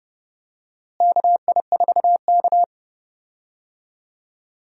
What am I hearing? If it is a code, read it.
Morse code, heard as a keyed tone, about 30 wpm, 700 Hz. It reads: KI4K
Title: USA Call Signs - 4 Characters - Repeated 3x - 30wpm